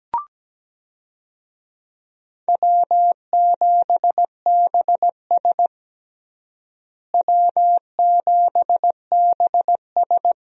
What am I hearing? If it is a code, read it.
Morse code, heard as a keyed tone, about 17 wpm, 700 Hz. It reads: W7BS W7BS
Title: USA Call Signs - 4 Characters - Repeated 2x - 17wpm